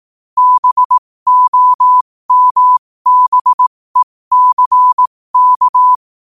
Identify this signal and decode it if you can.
Morse code, heard as a keyed tone, about 18 wpm, spaced as for 15 wpm, 1000 Hz. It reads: BOMBECK
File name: qotdfiles / 15wpm